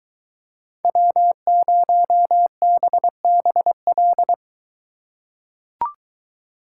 Morse code, heard as a keyed tone, about 22 wpm, 700 Hz. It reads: W0BBL